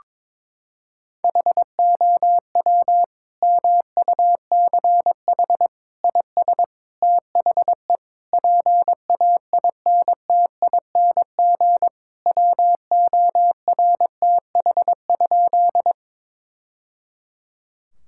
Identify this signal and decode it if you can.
Morse code, heard as a keyed tone, about 22 wpm, 700 Hz. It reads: HOW MUCH IS THE PAINTING WORTH?
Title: Sentences from Top 1000 Words - Review Part 6 - 22wpm